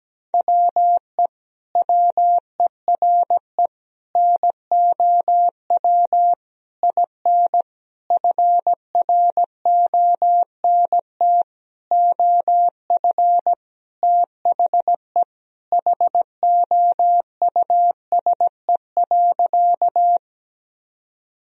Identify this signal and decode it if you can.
Morse code, heard as a keyed tone, about 17 wpm, 700 Hz. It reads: WE WERE NOW IN FRONT OF THE HOUSE.
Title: Sentences from Top 1000 Words - Review Part 2 - 17wpm